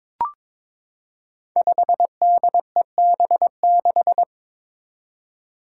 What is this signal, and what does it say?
Morse code, heard as a keyed tone, about 21 wpm, 700 Hz. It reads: 5DEB6
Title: Five Challenging Letter-Number Combinations 22wpm